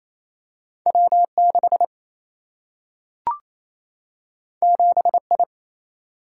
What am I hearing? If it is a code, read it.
Morse code, heard as a keyed tone, about 28 wpm, 700 Hz. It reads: W6 7I
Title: Single-Letter Single-Number 28wpm